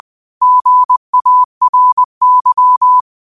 Morse code, heard as a keyed tone, about 20 wpm, 1000 Hz. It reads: GARY